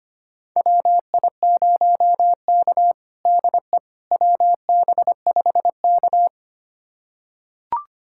Morse code, heard as a keyed tone, about 25 wpm, 700 Hz. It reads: WI0K DE WB5K
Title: USA Callsign DE Callsign - 4 Characters - 25wpm